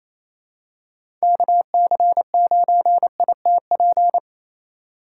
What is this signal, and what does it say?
Morse code, heard as a keyed tone, about 28 wpm, 700 Hz. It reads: KC9ITP
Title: USA Call Signs - 6 Characters - 2x Character Spacing - 28wpm